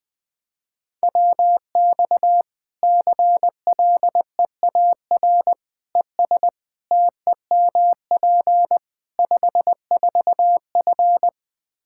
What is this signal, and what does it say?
Morse code, heard as a keyed tone, about 21 wpm, 700 Hz. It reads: WX CLEAR ES TEMP 54F
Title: Practice QSO - Exchange 2 - Info - 20wpm